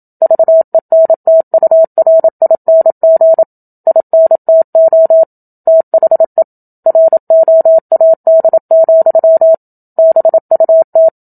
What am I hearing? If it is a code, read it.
Morse code, heard as a keyed tone, about 27 wpm, 650 Hz. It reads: VENTURING INTO THE ROAD, BUT
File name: Book Two / 01-Chapter 1 - 27 WPM